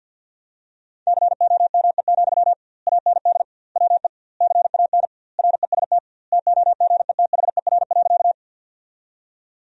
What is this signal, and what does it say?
Morse code, heard as a keyed tone, about 50 wpm, 700 Hz. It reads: COME, AND WE CAN REST TOGETHER.
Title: 2x Word-Spacing - Sentences from Top 500 Words - 50wpm